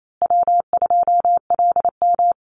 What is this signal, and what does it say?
Morse code, heard as a keyed tone, about 28 wpm, 700 Hz. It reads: W2LM
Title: USA Call Signs Repeated 3x 28wpm